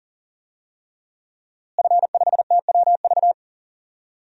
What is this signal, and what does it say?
Morse code, heard as a keyed tone, about 40 wpm, 700 Hz. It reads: F5TWV
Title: DXCC-100 Call Signs - 5 Characters - Repeated 3x - 40wpm